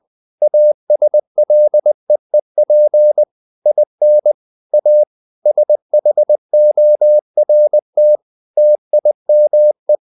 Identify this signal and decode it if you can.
Morse code, heard as a keyed tone, about 20 wpm, 600 Hz. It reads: ASLEEP IN A SHORT TIME
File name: the old man and the sea / the old man and the sea part 1 20wpm